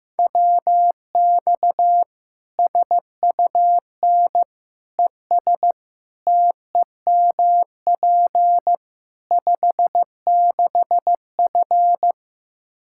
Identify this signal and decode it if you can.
Morse code, heard as a keyed tone, about 15 wpm, 700 Hz. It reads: WX SUN ES TEMP 56F
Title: Practice QSO - Exchange 2 - Info - 15wpm